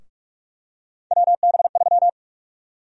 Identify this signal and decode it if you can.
Morse code, heard as a keyed tone, about 46 wpm, 700 Hz. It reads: WB3
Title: USA Call Sign Prefixes - Repeated 2x - 45wpm